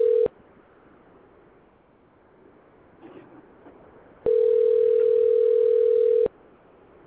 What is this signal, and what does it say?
Ringback tone (North America), 440 + 480 Hz.